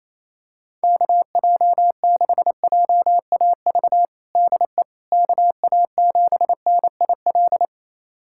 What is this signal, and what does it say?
Morse code, heard as a keyed tone, about 28 wpm, 700 Hz. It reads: KJ6JAV DE KA7NIL
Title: USA Callsign DE Callsign 28wpm